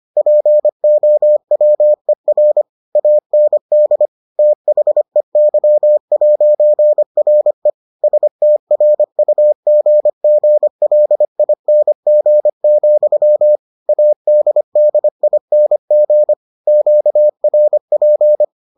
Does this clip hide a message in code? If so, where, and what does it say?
Yes: Morse code from 0.2 to 18.4 s, heard as a keyed tone, about 25 wpm, 600 Hz. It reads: POWER AND THEY'RE STRUGGLING, ADDING QRP